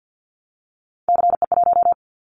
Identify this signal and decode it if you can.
Morse code, heard as a keyed tone, about 50 wpm, 700 Hz. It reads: CE1
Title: Callsign Prefixes for Top 100 DXCC - 50wpm